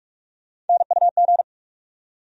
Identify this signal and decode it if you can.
Morse code, heard as a keyed tone, about 41 wpm, 700 Hz. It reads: NUG
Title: USA Call Sign Postfixes - 3 Letters 45wpm